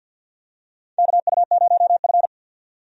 Morse code, heard as a keyed tone, about 52 wpm, 700 Hz. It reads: KU0F
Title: USA Call Signs - 4 Characters - 50wpm